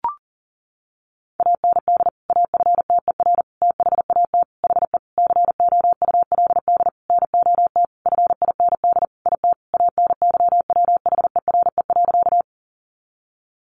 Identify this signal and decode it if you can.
Morse code, heard as a keyed tone, about 40 wpm, 700 Hz. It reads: AND AFTER THAT HE COULD NOT FIND IT ANYWHERE.
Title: Sentences from Top 1400 Words 40wpm